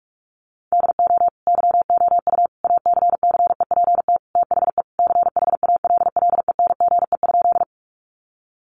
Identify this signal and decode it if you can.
Morse code, heard as a keyed tone, about 44 wpm, 700 Hz. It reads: DO YOU ACCEPT THE CHALLENGE?
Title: Sentences from Top 1300 Words 45wpm